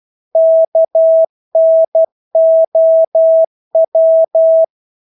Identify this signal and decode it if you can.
Morse code, heard as a keyed tone, about 12 wpm, 650 Hz. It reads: KNOW
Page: Engelske ord 12 wpm